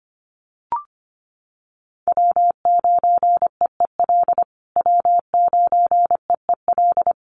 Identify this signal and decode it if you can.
Morse code, heard as a keyed tone, about 25 wpm, 700 Hz. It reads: W9EEL W9EEL
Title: USA Call Signs Repeated 2x 25wpm